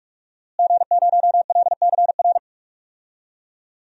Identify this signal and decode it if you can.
Morse code, heard as a keyed tone, about 46 wpm, 700 Hz. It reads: G0LCR
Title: DXCC-100 Call Signs - 5 Characters - Repeated 2x - 45wpm